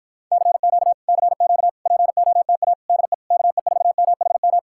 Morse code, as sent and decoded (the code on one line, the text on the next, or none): -.-. --.- / -.-. --.- / .--. --- - .- / -.. / -.- . ....- -- .... --
CQ CQ POTA DE KE4MHM